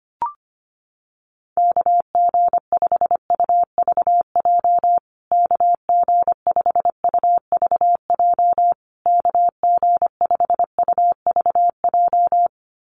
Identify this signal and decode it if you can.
Morse code, heard as a keyed tone, about 25 wpm, 700 Hz. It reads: KG5UVJ KG5UVJ KG5UVJ
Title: USA Call Signs Repeated 3x 25wpm